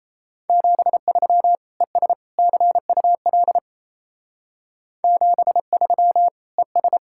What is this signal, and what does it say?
Morse code, heard as a keyed tone, about 37 wpm, 700 Hz. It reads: 73 ES CUL 73 ES